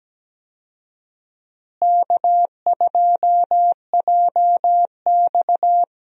Morse code, heard as a keyed tone, about 17 wpm, 700 Hz. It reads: K2JX
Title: USA Call Signs - 4 Characters - Repeated 2x - 17wpm